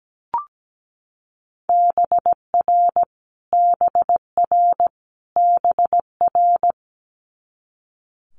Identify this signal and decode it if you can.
Morse code, heard as a keyed tone, about 17 wpm, 700 Hz. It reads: BR BR BR
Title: Multi-letter Phonemes Repeated 3x 17wpm